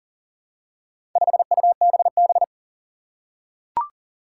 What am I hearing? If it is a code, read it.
Morse code, heard as a keyed tone, about 42 wpm, 700 Hz. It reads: 5UBB
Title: Four Challenging Letter-Number Combinations 40wpm